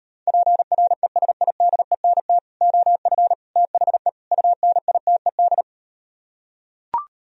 Morse code, heard as a keyed tone, about 38 wpm, 700 Hz. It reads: PRESIDENT OF THE UNITED